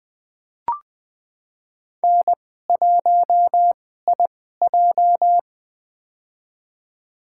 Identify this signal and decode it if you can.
Morse code, heard as a keyed tone, about 20 wpm, spaced as for 14 wpm, 700 Hz. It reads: N1IJ
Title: USA Call Signs - 4 Characters - 2x Character Spacing - 20wpm